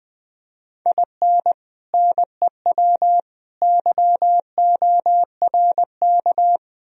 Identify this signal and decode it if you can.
Morse code, heard as a keyed tone, about 21 wpm, 700 Hz. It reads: IN NEW YORK